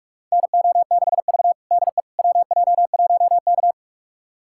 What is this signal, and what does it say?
Morse code, heard as a keyed tone, about 45 wpm, 700 Hz. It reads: NO6V DE WJ1K
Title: USA Callsign DE Callsign - 4 Characters - 45wpm